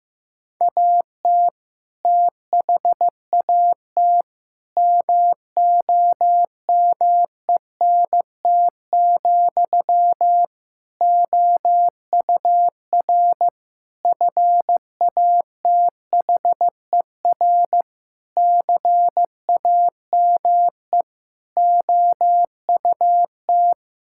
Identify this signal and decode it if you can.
Morse code, heard as a keyed tone, about 15 wpm, 700 Hz. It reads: AT THAT MOMENT, OUR FATHER CAME OUT
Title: Sentences from Top 1600 Words 15wpm